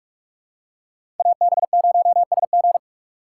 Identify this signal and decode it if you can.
Morse code, heard as a keyed tone, about 47 wpm, 700 Hz. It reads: AB0SG
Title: USA Call Signs - 5 Characters - 6x Character Spacing - 45wpm